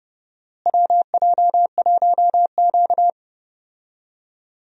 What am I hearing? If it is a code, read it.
Morse code, heard as a keyed tone, about 30 wpm, 700 Hz. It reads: WJ1Q
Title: USA Call Signs - 4 Characters - Repeated 2x - 30wpm